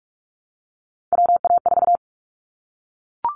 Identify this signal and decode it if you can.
Morse code, heard as a keyed tone, about 44 wpm, 700 Hz. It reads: WA4
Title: Callsign Prefixes for Top 25 DXCC - Repeated 3x - 45wpm